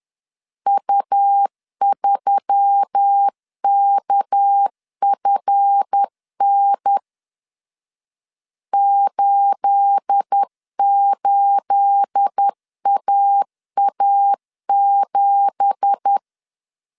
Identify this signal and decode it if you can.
Morse code, heard as a keyed tone, about 10 wpm, 800 Hz. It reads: U3KFN 88AA7
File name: CW / DET / 1 9 1500C S A 10wpm 4511